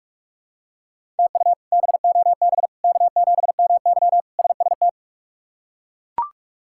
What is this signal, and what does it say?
Morse code, heard as a keyed tone, about 41 wpm, 700 Hz. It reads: TU BOB K7MY SST
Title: K1USN SST QSO Practice - 45wpm